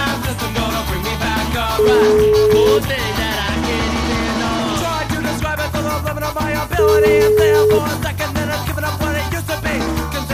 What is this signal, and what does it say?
Ringback tone (Europe, 425 Hz), with noise.